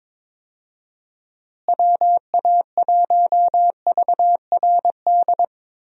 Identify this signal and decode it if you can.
Morse code, heard as a keyed tone, about 23 wpm, 700 Hz. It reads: WA1VRD
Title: USA Call Signs - 6 Characters - Repeated 3x - 22wpm